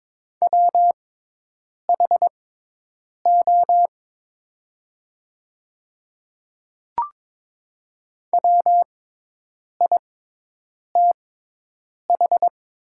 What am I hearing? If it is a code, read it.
Morse code, heard as a keyed tone, about 22 wpm, spaced as for 8 wpm, 700 Hz. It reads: WHO WITH
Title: 6x Character Spacing - Top 100 Words - 22wpm